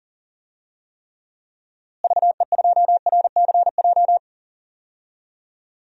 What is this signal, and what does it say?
Morse code, heard as a keyed tone, about 40 wpm, 700 Hz. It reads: VE2RCJ
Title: DXCC-100 Call Signs - 6 Characters - Repeated 2x - 40wpm